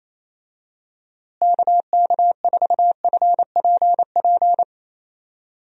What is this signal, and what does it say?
Morse code, heard as a keyed tone, about 28 wpm, 700 Hz. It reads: KK4FPP
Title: USA Call Signs - 6 Characters - Repeated 3x - 28wpm